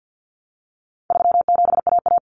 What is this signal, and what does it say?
Morse code, heard as a keyed tone, about 50 wpm, 700 Hz. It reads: 37AA
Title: Common Cut Numbers - 4-Digit 50wpm